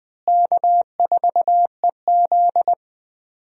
Morse code, heard as a keyed tone, about 20 wpm, 700 Hz. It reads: K4EZ
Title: USA Call Signs - 4 Characters - Repeated 3x - 20wpm